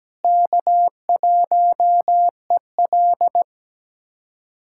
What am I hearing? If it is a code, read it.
Morse code, heard as a keyed tone, about 17 wpm, 700 Hz. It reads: K1EL
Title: USA Callsign DE Callsign - 4 Characters - 17wpm